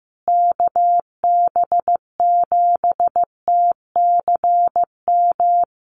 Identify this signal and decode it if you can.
Morse code, heard as a keyed tone, about 15 wpm, 700 Hz. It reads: KB7TCM
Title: USA Call Signs Repeated 2x 15wpm